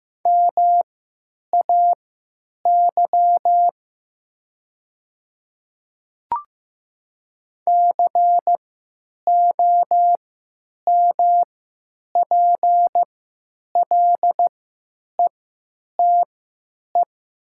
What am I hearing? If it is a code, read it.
Morse code, heard as a keyed tone, about 15 wpm, spaced as for 9 wpm, 700 Hz. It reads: MAY COMPLETE
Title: 3x Character Spacing - Top 500 Words - 15wpm